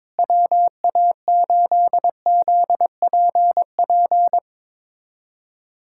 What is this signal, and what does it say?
Morse code, heard as a keyed tone, about 21 wpm, 700 Hz. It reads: WA8ZPP
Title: USA Callsign DE Callsign - 6 Characters - 22wpm